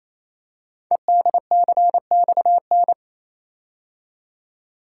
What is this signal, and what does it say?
Morse code, heard as a keyed tone, about 27 wpm, 700 Hz. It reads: EDCXN
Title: Code Groups - 5 Letters - 28wpm